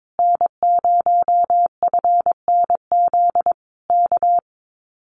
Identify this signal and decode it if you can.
Morse code, heard as a keyed tone, about 22 wpm, 700 Hz. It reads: N0FNZ K
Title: Common QSO Phrases Encore 22wpm